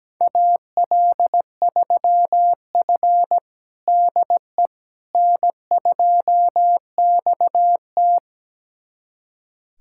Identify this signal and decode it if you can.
Morse code, heard as a keyed tone, about 17 wpm, 700 Hz. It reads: AL3F DE N2XT
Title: USA Callsign DE Callsign - 4 Characters - 17wpm